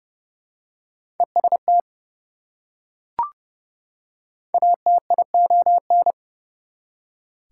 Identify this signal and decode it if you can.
Morse code, heard as a keyed tone, about 29 wpm, 700 Hz. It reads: EST ATION
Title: Prefixes and Suffixes 30wpm